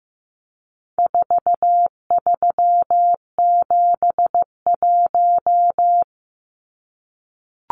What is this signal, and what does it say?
Morse code, heard as a keyed tone, about 15 wpm, 700 Hz. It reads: 4371
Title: Numbers - 4-Digit 15wpm